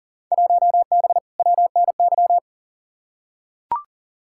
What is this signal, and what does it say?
Morse code, heard as a keyed tone, about 40 wpm, 700 Hz. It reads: 1B WNY
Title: Field Day Exchanges - 40wpm